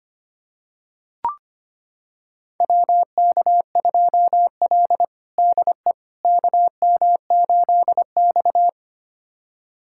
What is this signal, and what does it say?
Morse code, heard as a keyed tone, about 25 wpm, 700 Hz. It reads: WK2L DE KM8X
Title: USA Callsign DE Callsign - 4 Characters - 25wpm